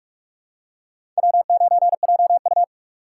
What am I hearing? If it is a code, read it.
Morse code, heard as a keyed tone, about 46 wpm, 700 Hz. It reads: W9JU
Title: USA Call Signs - 4 Characters - 3x Character Spacing - 45wpm